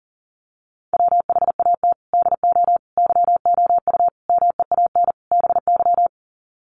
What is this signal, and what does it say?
Morse code, heard as a keyed tone, about 40 wpm, 700 Hz. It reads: WHAT DO YOU MEAN BY
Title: Sets of 5 Words 40wpm